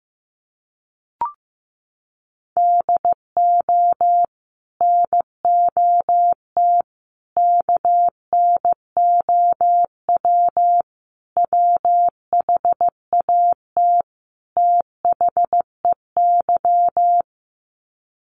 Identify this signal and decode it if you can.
Morse code, heard as a keyed tone, about 15 wpm, 700 Hz. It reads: DO NOT KNOW WHAT THEY
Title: Sets of 5 Words 15wpm